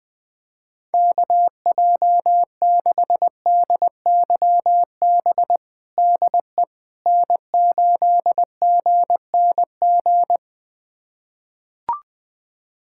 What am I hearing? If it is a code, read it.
Morse code, heard as a keyed tone, about 20 wpm, 700 Hz. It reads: KJ6DYB DE N8GNG
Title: USA Callsign DE Callsign 20wpm